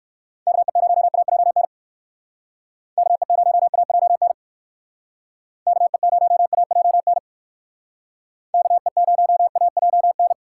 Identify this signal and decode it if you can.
Morse code, heard as a keyed tone, about 58 wpm, 700 Hz. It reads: KE0AJN KE0AJN KE0AJN KE0AJN